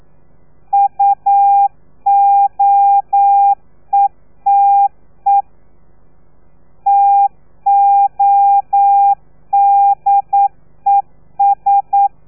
Morse code, heard as a keyed tone, about 9 wpm, 800 Hz. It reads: UOETE TODES
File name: telegrafia / i2tfj / Lez3 40